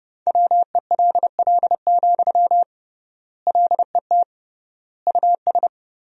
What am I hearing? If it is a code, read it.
Morse code, heard as a keyed tone, about 30 wpm, 700 Hz. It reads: WELL, LET US
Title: 3x Word-Spacing - Sentences from Top 300 Words - 30wpm